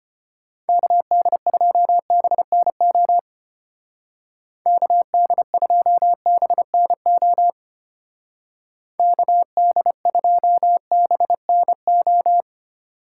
Morse code, heard as a keyed tone, about 32 wpm, 700 Hz. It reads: KD2BNO KD2BNO KD2BNO